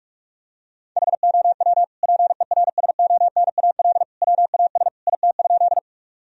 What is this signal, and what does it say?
Morse code, heard as a keyed tone, about 46 wpm, 700 Hz. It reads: HOW PERSONAL WAS IT?
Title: Sentences from Top 1000 Words - Review Part 4 - 45wpm